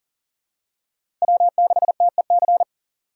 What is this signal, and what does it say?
Morse code, heard as a keyed tone, about 40 wpm, 700 Hz. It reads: W6TEC
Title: USA Call Signs - 5 Characters - 2x Character Spacing - 40wpm